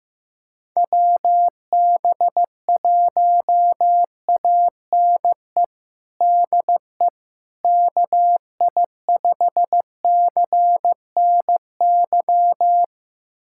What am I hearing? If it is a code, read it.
Morse code, heard as a keyed tone, about 15 wpm, 700 Hz. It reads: WB1ANE DE KI5CNY
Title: USA Callsign DE Callsign - 6 Characters - 15wpm